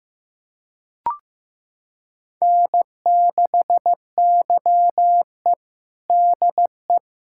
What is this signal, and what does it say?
Morse code, heard as a keyed tone, about 15 wpm, 700 Hz. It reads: N6YE DE
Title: USA Callsign DE Callsign - 4 Characters - 15wpm